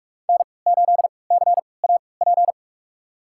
Morse code, heard as a keyed tone, about 41 wpm, spaced as for 25 wpm, 700 Hz. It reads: N8CAP
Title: USA Call Signs - 5 Characters - 3x Character Spacing - 45wpm